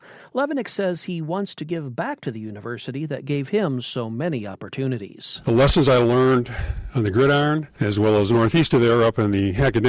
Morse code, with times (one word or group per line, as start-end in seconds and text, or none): none